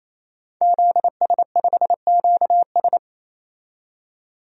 Morse code, heard as a keyed tone, about 28 wpm, 700 Hz. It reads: ZS5QS
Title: DXCC-100 Call Signs - 5 Characters - 28wpm